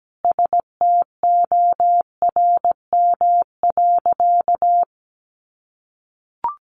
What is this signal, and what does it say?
Morse code, heard as a keyed tone, about 17 wpm, 700 Hz. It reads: STORM.
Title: Sentences from Top 1500 Words 17wpm